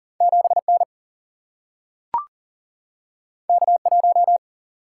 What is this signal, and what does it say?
Morse code, heard as a keyed tone, about 40 wpm, 700 Hz. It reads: 7 N K 1
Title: Single-Letter Single-Number 40wpm